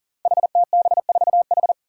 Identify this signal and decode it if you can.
Morse code, heard as a keyed tone, about 42 wpm, 700 Hz. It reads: HTB4H
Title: Five Challenging Letter-Number Combinations 40wpm